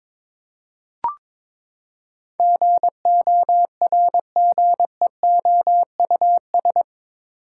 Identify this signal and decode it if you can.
Morse code, heard as a keyed tone, about 22 wpm, 700 Hz. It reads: GORGEOUS
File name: practice / 22wpm / Eight Letter Words - 22wpm